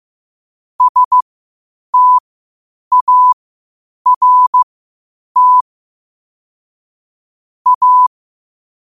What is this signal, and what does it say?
Morse code, heard as a keyed tone, about 15 wpm, spaced as for 8 wpm, 1000 Hz. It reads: START A
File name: qotdfiles / 8wpm